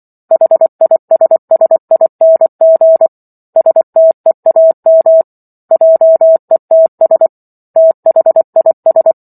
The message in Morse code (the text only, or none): HISSING STEAM JETS THIS